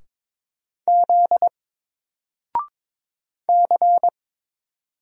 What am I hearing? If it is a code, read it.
Morse code, heard as a keyed tone, about 22 wpm, 700 Hz. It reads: Z C
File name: Single-Letter-Number-22wpm